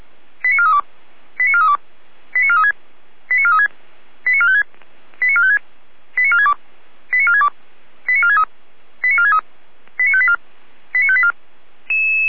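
Five-tone selective call (ZVEI), heard as quick runs of five tones, tones of about 70 ms. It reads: 89432 89432 89436 89436 89456 89456 89462 89462 89463 89463 89584 89584
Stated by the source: ZVEI five-tone call 89416, received on SDR, recorded by feuerwehr-orlamuende.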